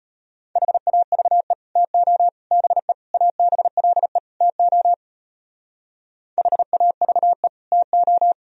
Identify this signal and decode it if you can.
Morse code, heard as a keyed tone, about 33 wpm, 700 Hz. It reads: HAVE TO BE ABLE TO HAVE TO